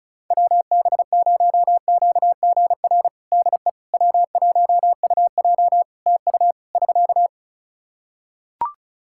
Morse code, heard as a keyed tone, about 34 wpm, 700 Hz. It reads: WB0QGR DE W1UJ TU <SK>
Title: Practice QSO - Exchange 4 - Ending - 35wpm